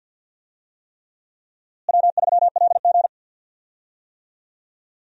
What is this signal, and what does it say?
Morse code, heard as a keyed tone, about 52 wpm, 700 Hz. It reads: W3LG
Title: USA Call Signs - 4 Characters - Repeated 3x - 50wpm